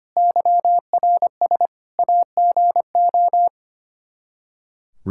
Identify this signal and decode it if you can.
Morse code, heard as a keyed tone, about 25 wpm, 700 Hz. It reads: YRS AGO
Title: K1USN SST QSO Practice 25wpm